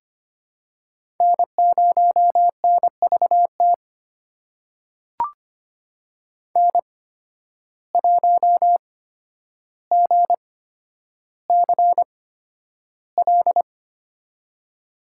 Morse code, heard as a keyed tone, about 25 wpm, spaced as for 11 wpm, 700 Hz. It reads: N0NVT N1GCL
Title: USA Call Signs - 8x Character Spacing - 25wpm